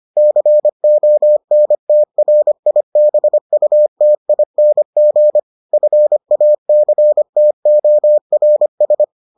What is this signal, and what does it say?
Morse code, heard as a keyed tone, about 25 wpm, 600 Hz. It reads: CONTRIBUTING FACTORS